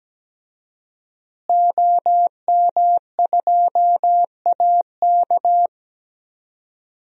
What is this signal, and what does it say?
Morse code, heard as a keyed tone, about 17 wpm, 700 Hz. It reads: OM2AK
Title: DXCC-100 Call Signs - 5 Characters - Repeated 2x - 17wpm